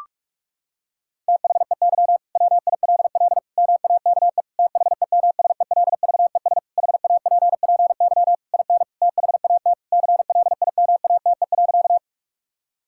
Morse code, heard as a keyed tone, about 41 wpm, 700 Hz. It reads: THEY WILL MAKE THEMSELVES HAPPY IN THAT CLIMATE.
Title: Sentences from Top 1600 Words 45wpm